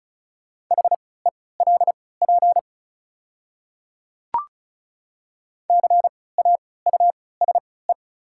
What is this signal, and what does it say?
Morse code, heard as a keyed tone, about 36 wpm, spaced as for 20 wpm, 700 Hz. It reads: HELP CAUSE
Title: 3x Character Spacing - Top 200 Words - 35wpm